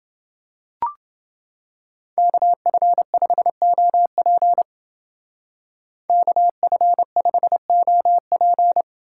Morse code, heard as a keyed tone, about 29 wpm, 700 Hz. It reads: KF5OP KF5OP